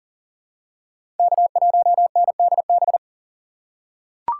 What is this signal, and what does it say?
Morse code, heard as a keyed tone, about 40 wpm, 700 Hz. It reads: K1NDB
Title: USA Call Signs - 5 Characters - 6x Character Spacing - 40wpm